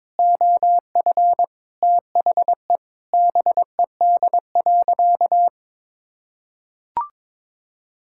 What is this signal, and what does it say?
Morse code, heard as a keyed tone, about 22 wpm, 700 Hz. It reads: OF THE BED.
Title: Sentences from Top 1000 Words - Review Part 6 - 22wpm